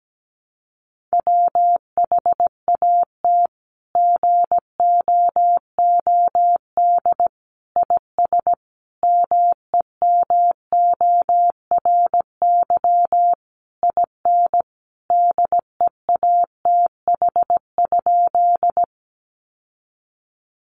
Morse code, heard as a keyed tone, about 17 wpm, 700 Hz. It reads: WHAT GOOD IS MEMORY IN DEATH?
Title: Sentences from Top 1300 Words 17wpm